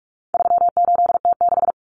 Morse code, heard as a keyed tone, about 44 wpm, 700 Hz. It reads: 38T6
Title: Common Cut Numbers - 4-Digit 45wpm